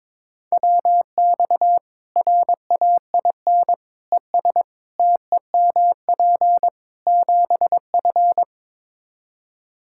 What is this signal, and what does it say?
Morse code, heard as a keyed tone, about 22 wpm, 700 Hz. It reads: WX RAIN ES TEMP 7F
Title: Practice QSO - Exchange 2 - Info - 22wpm